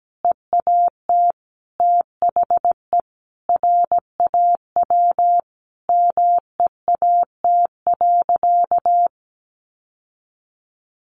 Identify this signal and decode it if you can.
Morse code, heard as a keyed tone, about 17 wpm, 700 Hz. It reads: EAT THE RAW MEAT.
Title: Sentences from Top 1500 Words 17wpm